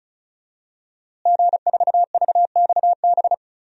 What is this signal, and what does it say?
Morse code, heard as a keyed tone, about 36 wpm, 700 Hz. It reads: G4VXB
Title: DXCC-100 Call Signs - 5 Characters - Repeated 2x - 35wpm